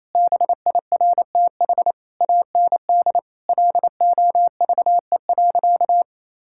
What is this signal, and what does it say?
Morse code, heard as a keyed tone, about 28 wpm, 700 Hz. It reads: BIRTH AND LOVE.
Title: Sentences from Top 1000 Words - Review Part 5 - 28wpm